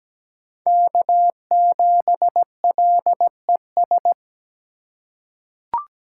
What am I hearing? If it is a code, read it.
Morse code, heard as a keyed tone, about 17 wpm, 700 Hz. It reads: K7LES
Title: USA Call Signs - 3x Character Spacing - 17wpm